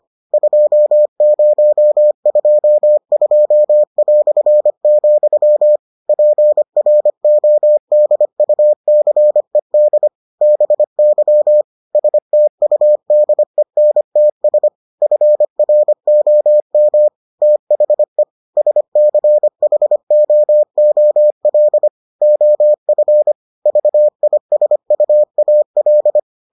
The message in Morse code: 2022", PRODUCED BY STUDENTS FROM THE SCHOOL OF VISUAL

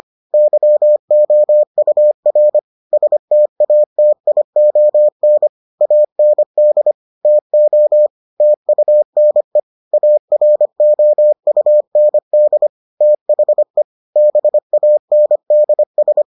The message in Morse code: YOUR STATION AND TO TUNE AROUND THE BANDS

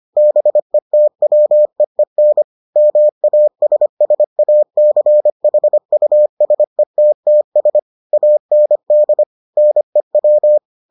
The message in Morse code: BETWEEN MASSACHUSETTS AND NEW

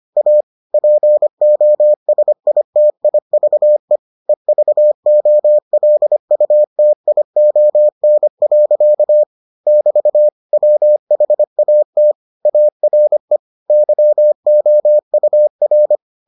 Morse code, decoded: A POSITIVE EVOLUTION. <BT> WHAT ARE YOUR

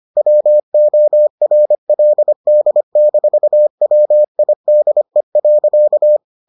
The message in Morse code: WORLD-WIDE.

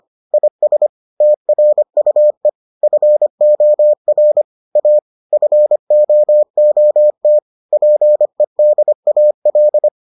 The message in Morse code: IS TRUE FOR A FOOT PEDAL

